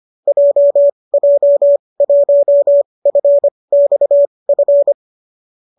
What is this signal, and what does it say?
Morse code, heard as a keyed tone, about 25 wpm, spaced as for 20 wpm, 570 Hz. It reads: JJ1FXF
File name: qso208 20